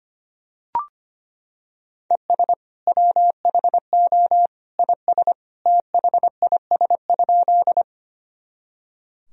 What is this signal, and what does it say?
Morse code, heard as a keyed tone, about 25 wpm, 700 Hz. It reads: ES WHO IS THIS?
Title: Sentences from Top 100 Words and Abbreviations 25wpm